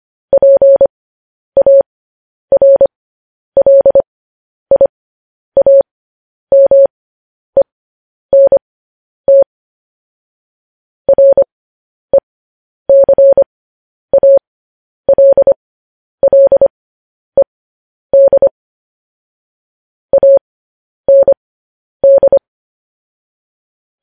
Morse code, heard as a keyed tone, about 25 wpm, spaced as for 10 wpm, 550 Hz. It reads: PARLIAMENT RECALLED AND
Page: news headlines in morse code at 10 wpm